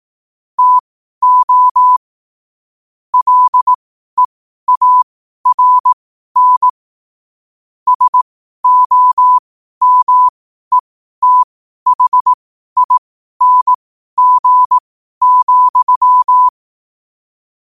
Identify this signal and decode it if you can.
Morse code, heard as a keyed tone, about 18 wpm, spaced as for 12 wpm, 1000 Hz. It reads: TO LEARN SOMETHING,
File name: qotdfiles / 12wpm